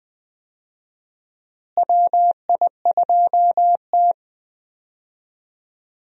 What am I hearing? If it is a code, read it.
Morse code, heard as a keyed tone, about 20 wpm, 700 Hz. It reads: WI2T
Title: USA Call Signs - 4 Characters - Repeated 2x - 20wpm